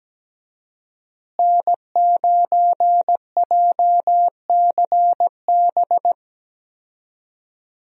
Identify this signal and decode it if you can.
Morse code, heard as a keyed tone, about 17 wpm, 700 Hz. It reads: N9JCB
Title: USA Call Signs - 5 Characters - Repeated 3x - 17wpm